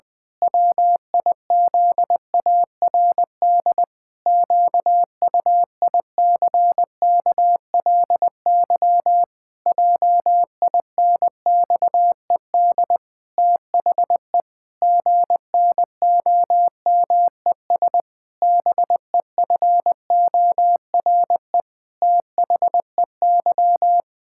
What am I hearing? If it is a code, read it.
Morse code, heard as a keyed tone, about 20 wpm, 700 Hz. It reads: WIZARD QUICKLY JINXED THE GNOMES BEFORE THEY
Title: Pangram Challenge v2 - 20wpm